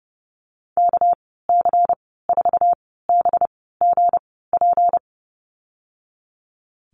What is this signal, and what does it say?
Morse code, heard as a keyed tone, about 30 wpm, spaced as for 17 wpm, 700 Hz. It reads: KC4BGP
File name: Lesson 37g - Letter x - Call Signs 30wpm